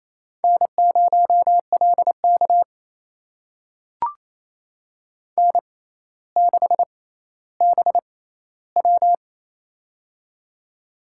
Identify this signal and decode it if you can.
Morse code, heard as a keyed tone, about 28 wpm, spaced as for 14 wpm, 700 Hz. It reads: N0LK N6BW
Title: USA Call Signs - 4 Characters - 6x Character Spacing - 28wpm